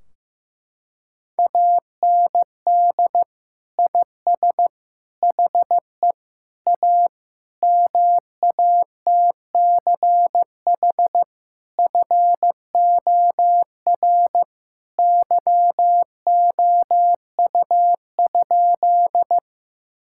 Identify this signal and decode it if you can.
Morse code, heard as a keyed tone, about 15 wpm, 700 Hz. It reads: AND IS HE A MATCH FOR YOU?
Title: Sentences from Top 1600 Words 15wpm